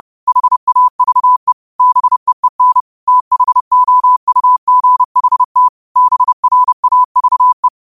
Morse code, heard as a keyed tone, about 30 wpm, 1000 Hz. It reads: HAVE BEEN THOUGHT BRAVE
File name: qotdfiles / 30wpm